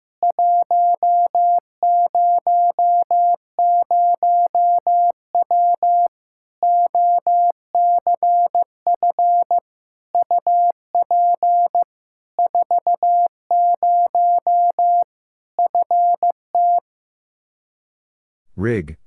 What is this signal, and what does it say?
Morse code, heard as a keyed tone, about 15 wpm, 700 Hz. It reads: 100W OCF UP 40 FT
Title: K1USN SST QSO Practice 15wpm